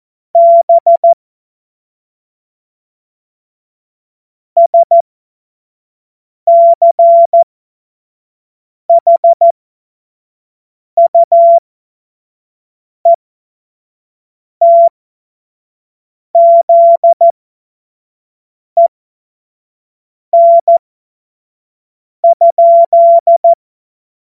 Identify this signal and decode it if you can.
Morse code, heard as a keyed tone, about 14 wpm, spaced as for 5 wpm, 680 Hz. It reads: B SCHUETZEN?
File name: Morsekurs / Witze / Witz 17. August 2021, 25 F70